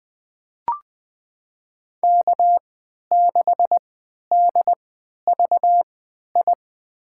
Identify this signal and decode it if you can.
Morse code, heard as a keyed tone, about 20 wpm, spaced as for 11 wpm, 700 Hz. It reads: K6DVI
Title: USA Call Signs - 5 Characters - 3x Character Spacing - 20wpm